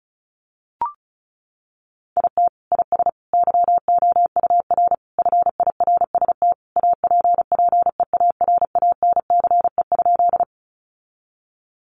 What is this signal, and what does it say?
Morse code, heard as a keyed tone, about 34 wpm, 700 Hz. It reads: IT IS YOUR FIRST APPEARANCE?
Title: Sentences from Top 1100 Words 35wpm